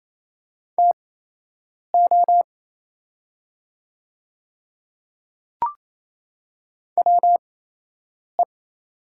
Morse code, heard as a keyed tone, about 28 wpm, spaced as for 8 wpm, 700 Hz. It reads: TO WE
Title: 8x Character Spacing - Top 100 Words - 28wpm